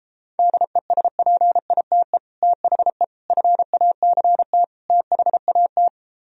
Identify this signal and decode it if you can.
Morse code, heard as a keyed tone, about 36 wpm, 700 Hz. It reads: DESPITE THE FACT THAT